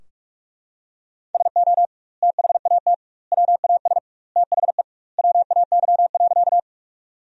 Morse code, heard as a keyed tone, about 42 wpm, 700 Hz. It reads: SO THAT WAS THE WAY.
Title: 2x Word-Spacing - Sentences from Top 100 Words - 45wpm